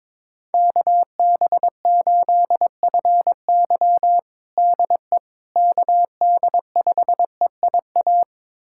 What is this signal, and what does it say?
Morse code, heard as a keyed tone, about 22 wpm, 700 Hz. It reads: KB8FY DE KD5EIA
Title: USA Callsign DE Callsign 22wpm